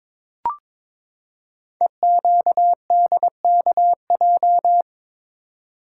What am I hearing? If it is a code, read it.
Morse code, heard as a keyed tone, about 22 wpm, 700 Hz. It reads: EQDKJ
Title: Code Groups - 5 Letters - 22wpm